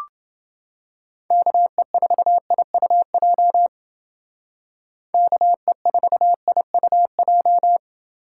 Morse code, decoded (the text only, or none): KE4IUJ KE4IUJ